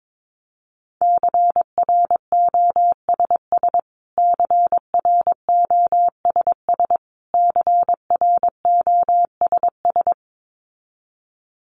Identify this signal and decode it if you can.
Morse code, heard as a keyed tone, about 22 wpm, 700 Hz. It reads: CROSS CROSS CROSS
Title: Top 500 Words Repeated 22wpm